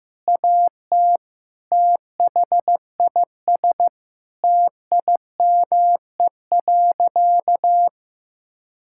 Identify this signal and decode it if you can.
Morse code, heard as a keyed tone, about 15 wpm, 700 Hz. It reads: AT THIS TIME.
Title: Sentences from Top 1000 Words - Review Part 5 - 15wpm